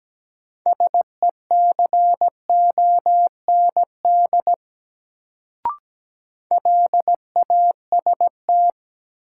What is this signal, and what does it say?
Morse code, heard as a keyed tone, about 17 wpm, 700 Hz. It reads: SECOND LAST
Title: Sentences from Top 300 Words - New Word Drill - 17wpm